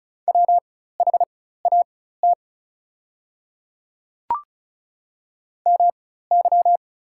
Morse code, heard as a keyed tone, about 34 wpm, spaced as for 16 wpm, 690 Hz. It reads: WHAT MY
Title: 4x Character Spacing - Top 100 Words - 35wpm